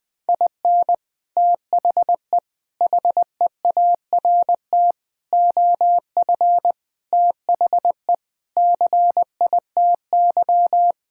Morse code, heard as a keyed tone, about 20 wpm, 700 Hz. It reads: IN THE HEART OF THE CITY
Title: Sentences with 3-Word Phrases - 20wpm